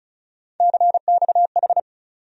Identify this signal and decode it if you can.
Morse code, heard as a keyed tone, about 34 wpm, 700 Hz. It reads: CXH
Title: USA Call Sign Postfixes - 3 Letters 35wpm